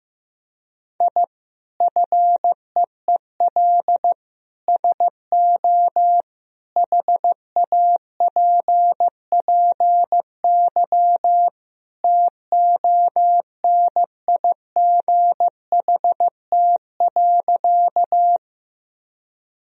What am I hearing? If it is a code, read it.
Morse code, heard as a keyed tone, about 15 wpm, 700 Hz. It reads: I FEEL SO HAPPY TONIGHT.
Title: Sentences from Top 1000 Words - Review Part 3 - 15wpm